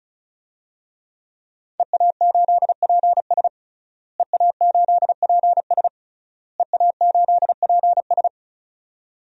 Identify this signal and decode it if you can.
Morse code, heard as a keyed tone, about 34 wpm, 690 Hz. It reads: EA8PS EA8PS EA8PS
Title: DXCC-100 Call Signs - 5 Characters - Repeated 3x - 35wpm